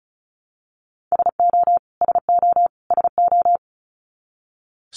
Morse code, heard as a keyed tone, about 34 wpm, 700 Hz. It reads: SO SO SO
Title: Top 200 Words Repeated 35wpm